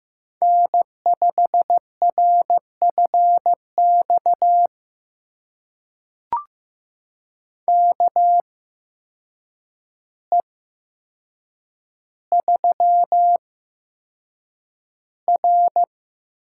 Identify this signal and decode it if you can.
Morse code, heard as a keyed tone, about 15 wpm, spaced as for 7 wpm, 700 Hz. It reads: N5RFX KE3R
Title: USA Call Signs - 8x Character Spacing - 15wpm